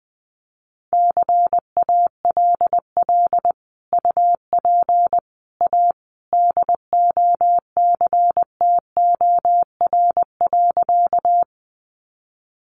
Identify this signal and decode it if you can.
Morse code, heard as a keyed tone, about 20 wpm, 700 Hz. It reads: CALL UP A DOCTOR.
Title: Sentences from Top 1500 Words 20wpm